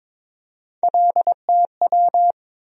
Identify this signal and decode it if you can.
Morse code, heard as a keyed tone, about 22 wpm, 700 Hz. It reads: LTW
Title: USA Call Sign Postfixes - 3 Letters 22wpm